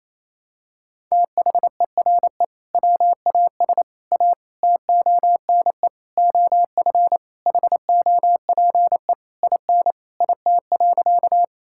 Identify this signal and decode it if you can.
Morse code, heard as a keyed tone, about 28 wpm, 700 Hz. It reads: THERE WAS A TONE OF HOPE IN IT.
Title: Sentences from Top 1600 Words 28wpm